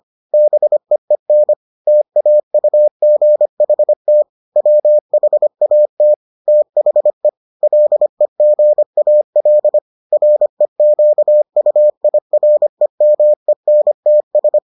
Morse code, heard as a keyed tone, about 25 wpm, 600 Hz. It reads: BEEN TAUGHT WHAT THE LEGAL REQUIREMENTS